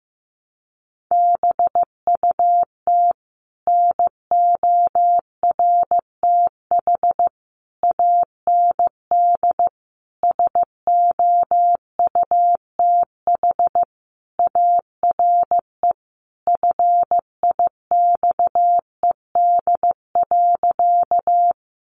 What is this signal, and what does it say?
Morse code, heard as a keyed tone, about 15 wpm, 700 Hz. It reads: BUT NORTH AND SOUTH ARE FIXED.
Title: Sentences from Top 1100 Words 15wpm